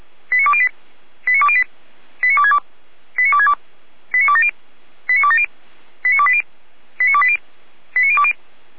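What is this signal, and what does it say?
Five-tone selective call (ZVEI), heard as quick runs of five tones, tones of about 70 ms. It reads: 89208 89208 89262 89262 89270 89270 89280 89280 89020
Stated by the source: ZVEI five-tone selective call 89340 received via SDR.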